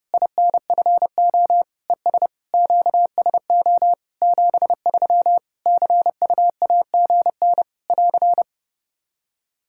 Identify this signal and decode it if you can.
Morse code, heard as a keyed tone, about 30 wpm, 700 Hz. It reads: INFO ES QSO 73 CUAGN <AR>
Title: K1USN SST QSO Practice 30wpm